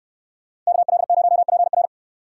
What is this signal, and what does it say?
Morse code, heard as a keyed tone, about 69 wpm, 690 Hz. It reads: KL0PR